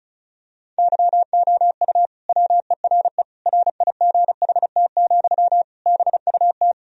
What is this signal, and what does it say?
Morse code, heard as a keyed tone, about 34 wpm, 700 Hz. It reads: YOU WERE RIGHT, BUT
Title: Sentences from Top 1000 Words - Review Part 2 - 35wpm